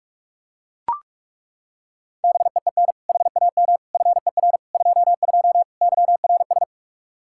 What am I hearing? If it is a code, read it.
Morse code, heard as a keyed tone, about 41 wpm, 700 Hz. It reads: BEEN HAM FER 22 YRS
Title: Practice QSO - Exchange 3 - More Info - 45wpm